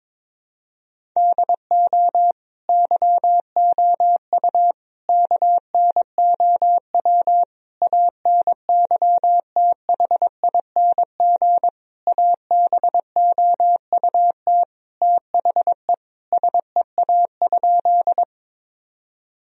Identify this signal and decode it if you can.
Morse code, heard as a keyed tone, about 22 wpm, 700 Hz. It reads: DO YOU KNOW ANYTHING ABOUT THE SEA?
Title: Sentences from Top 1000 Words - Review Part 1 - 22wpm